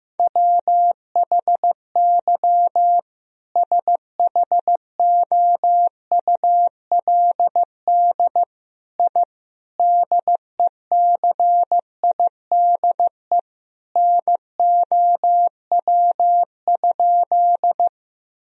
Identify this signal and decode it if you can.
Morse code, heard as a keyed tone, about 16 wpm, 700 Hz. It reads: WHY SHOULD I DECIDE NOW?
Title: Sentences from Top 1000 Words - Review Part 3 - 15wpm